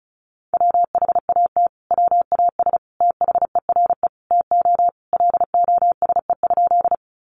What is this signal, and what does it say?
Morse code, heard as a keyed tone, about 34 wpm, 700 Hz. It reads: WHAT WAS THERE TO LOSE?
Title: Sentences from Top 1100 Words 35wpm